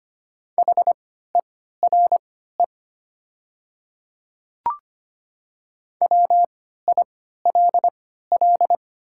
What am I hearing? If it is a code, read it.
Morse code, heard as a keyed tone, about 25 wpm, spaced as for 14 wpm, 700 Hz. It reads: HERE WILL
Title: 3x Character Spacing - Top 500 Words - 25wpm